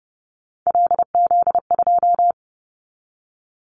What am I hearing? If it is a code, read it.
Morse code, heard as a keyed tone, about 30 wpm, 700 Hz. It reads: LZ2
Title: Callsign Prefixes for Top 100 DXCC - 30wpm